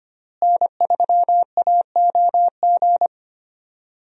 Morse code, heard as a keyed tone, about 25 wpm, 700 Hz. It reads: N3AOG
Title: USA Call Signs - 5 Characters - Repeated 2x - 25wpm